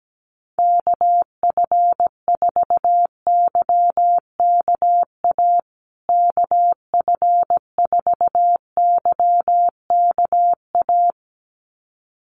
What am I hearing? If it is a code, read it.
Morse code, heard as a keyed tone, about 17 wpm, 700 Hz. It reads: KF4YKA KF4YKA
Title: USA Call Signs Repeated 2x 17wpm